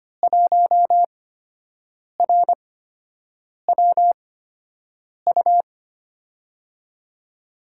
Morse code, heard as a keyed tone, about 25 wpm, spaced as for 7 wpm, 700 Hz. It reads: 1RWU
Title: USA Call Signs - 6 Characters - 8x Character Spacing - 25wpm